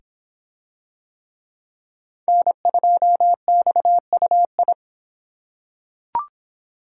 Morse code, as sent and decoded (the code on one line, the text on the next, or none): -. ..--- -..- ..- ..
N2XUI